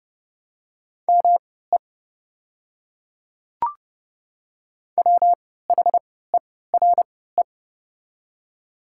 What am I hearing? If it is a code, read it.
Morse code, heard as a keyed tone, about 30 wpm, spaced as for 17 wpm, 700 Hz. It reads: ME WHERE
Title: 3x Character Spacing - Top 500 Words - 30wpm